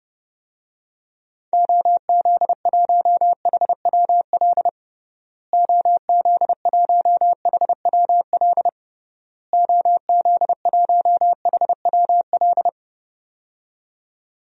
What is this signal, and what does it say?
Morse code, heard as a keyed tone, about 32 wpm, 700 Hz. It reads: OZ1HWL OZ1HWL OZ1HWL
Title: DXCC-100 Call Signs - 6 Characters - Repeated 3x - 30wpm